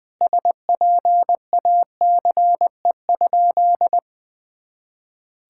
Morse code, heard as a keyed tone, about 20 wpm, 700 Hz. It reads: SPACE?
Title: Sentences from Top 1000 Words - Review Part 5 - 20wpm